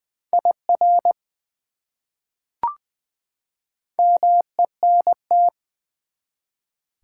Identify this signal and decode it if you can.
Morse code, heard as a keyed tone, about 20 wpm, 700 Hz. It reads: IR MENT
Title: Prefixes and Suffixes 20wpm